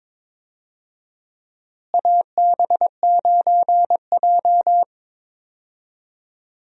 Morse code, heard as a keyed tone, about 21 wpm, 700 Hz. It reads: AB9J
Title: USA Call Signs - 4 Characters - Repeated 2x - 22wpm